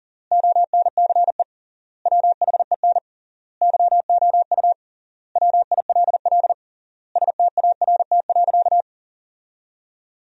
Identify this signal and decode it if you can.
Morse code, heard as a keyed tone, about 40 wpm, 700 Hz. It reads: ONCE WHEN YOU WILL START.
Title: 3x Word-Spacing - Sentences from Top 300 Words - 40wpm